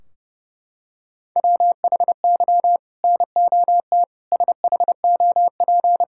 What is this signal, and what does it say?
Morse code, heard as a keyed tone, about 32 wpm, 700 Hz. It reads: WHY NOT SHOP
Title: Sentences from Top 1600 Words 30wpm